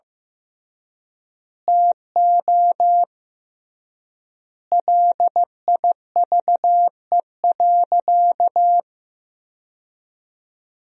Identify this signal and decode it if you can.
Morse code, heard as a keyed tone, about 15 wpm, 700 Hz. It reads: TO LIVE.
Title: 3x Word-Spacing - Sentences from Top 300 Words - 15wpm